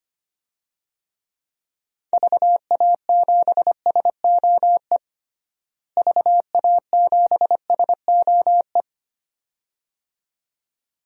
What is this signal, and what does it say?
Morse code, heard as a keyed tone, about 25 wpm, 700 Hz. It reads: VA7SOE VA7SOE
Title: DXCC-100 Call Signs - 6 Characters - Repeated 2x - 25wpm